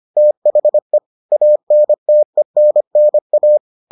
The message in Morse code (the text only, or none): THE ANTENNA